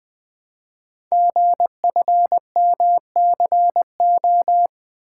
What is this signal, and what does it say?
Morse code, heard as a keyed tone, about 20 wpm, 700 Hz. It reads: GFMCO
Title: Code Groups - 5 Letters - 20wpm